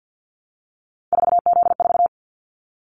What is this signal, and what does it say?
Morse code, heard as a keyed tone, about 50 wpm, 700 Hz. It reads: 4Z4
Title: Callsign Prefixes for Top 50 DXCC - 50wpm